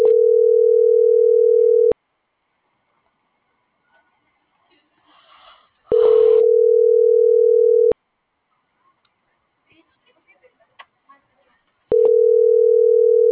Ringback tone (North America), 440 + 480 Hz.
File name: CENTRAL GENERAL / out america.mejia